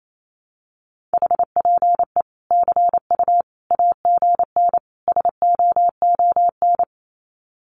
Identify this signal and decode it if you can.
Morse code, heard as a keyed tone, about 28 wpm, 700 Hz. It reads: HPE CU AGN SOON
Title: Common QSO Phrases 28wpm